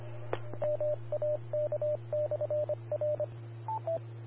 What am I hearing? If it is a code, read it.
Morse code, heard as a keyed tone, about 25 wpm, 600 Hz, with radio noise. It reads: MAK/R
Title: Austin, TX Amateur Repeaters